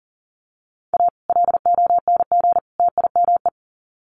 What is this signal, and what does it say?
Morse code, heard as a keyed tone, about 38 wpm, 700 Hz. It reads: A LONG TIME
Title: Sets of 3 Words 40wpm